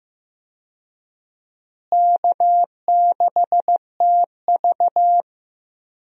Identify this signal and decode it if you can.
Morse code, heard as a keyed tone, about 15 wpm, 700 Hz. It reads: K6TV
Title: USA Call Signs - 4 Characters - Repeated 3x - 15wpm